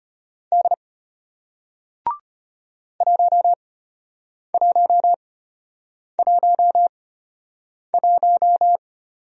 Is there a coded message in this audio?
Yes: Morse code from 0.5 to 8.8 s, heard as a keyed tone, about 34 wpm, spaced as for 8 wpm, 700 Hz. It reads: D 1111